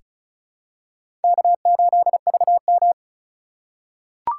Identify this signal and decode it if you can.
Morse code, heard as a keyed tone, about 36 wpm, 700 Hz. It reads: K8VM
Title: USA Call Signs - 2x Character Spacing - 35wpm